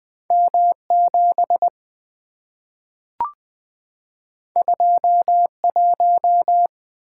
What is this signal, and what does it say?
Morse code, heard as a keyed tone, about 20 wpm, 700 Hz. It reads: M7 21
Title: Two Letter-Number Combinations - 20wpm